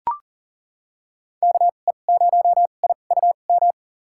Morse code, heard as a keyed tone, about 40 wpm, spaced as for 29 wpm, 700 Hz. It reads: KE0IUM
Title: USA Call Signs - 2x Character Spacing - 40wpm